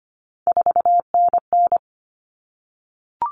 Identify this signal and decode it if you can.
Morse code, heard as a keyed tone, about 25 wpm, 700 Hz. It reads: 4NN
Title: Common Cut Numbers - 3-Digit 25wpm